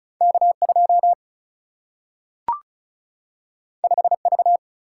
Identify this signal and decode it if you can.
Morse code, heard as a keyed tone, about 36 wpm, 700 Hz. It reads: K 2 5 V
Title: Single-Letter Single-Number 35wpm